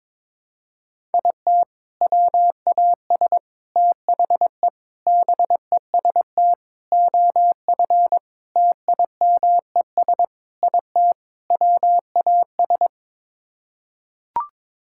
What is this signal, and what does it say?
Morse code, heard as a keyed tone, about 22 wpm, 700 Hz. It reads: IT WAS THE BEST OF TIMES IT WAS
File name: practice / 22wpm / Sets of 8 Words - 22wpm